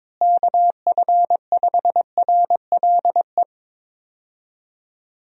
Morse code, heard as a keyed tone, about 22 wpm, 700 Hz. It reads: KF5RLE
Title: USA Call Signs - 6 Characters - Repeated 3x - 22wpm